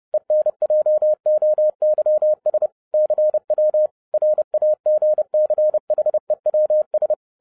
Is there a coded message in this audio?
Yes: Morse code from 0.1 to 7.1 s, heard as a keyed tone, about 30 wpm, 610 Hz. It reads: ENJOYS CW RAGCHEWS